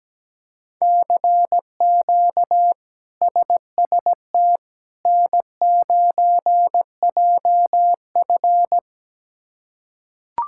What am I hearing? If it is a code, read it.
Morse code, heard as a keyed tone, about 17 wpm, 700 Hz. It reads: CQ SST N9JF
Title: K1USN SST QSO Practice - 17wpm